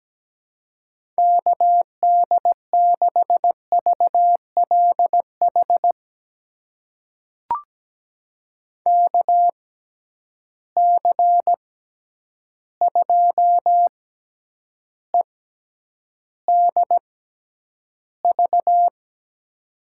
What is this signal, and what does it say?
Morse code, heard as a keyed tone, about 17 wpm, spaced as for 9 wpm, 700 Hz. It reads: KD6VLH KC2EDV
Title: USA Call Signs - 6 Characters - 6x Character Spacing - 17wpm